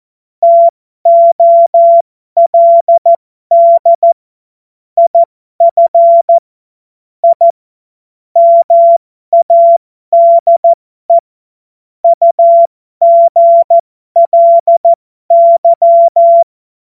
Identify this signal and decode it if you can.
Morse code, heard as a keyed tone, about 14 wpm, spaced as for 12 wpm, 680 Hz. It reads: TOLD IF I MADE UGLY
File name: Morsekurs / Witze